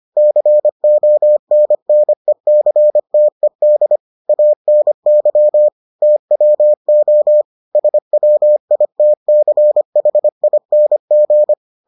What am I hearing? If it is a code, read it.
Morse code, heard as a keyed tone, about 25 wpm, 600 Hz. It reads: CONNECTED ANY TWO SWITCHING